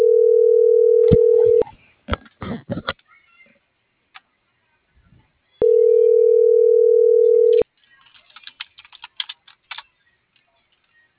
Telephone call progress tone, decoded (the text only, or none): ringback tone